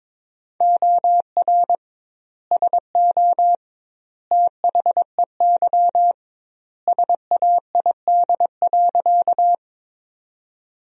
Morse code, heard as a keyed tone, about 21 wpm, 700 Hz. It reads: OR SO THEY SAID.
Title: 2x Word-Spacing - Sentences from Top 100 Words - 22wpm